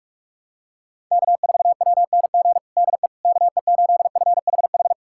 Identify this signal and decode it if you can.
Morse code, heard as a keyed tone, about 42 wpm, 700 Hz. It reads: K4WNG DE KE8FHH
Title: USA Callsign DE Callsign 45wpm